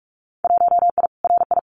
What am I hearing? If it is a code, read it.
Morse code, heard as a keyed tone, about 46 wpm, 700 Hz. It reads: 1I RI
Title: Winter Field Day Exchanges Encore 45wpm